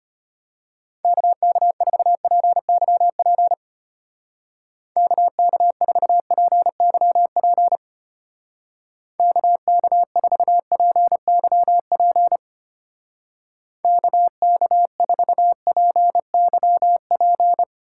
Morse code, heard as a keyed tone, about 31 wpm, 700 Hz. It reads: KK4PYP KK4PYP KK4PYP KK4PYP
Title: Speed Racing - USA Call Signs 25wpm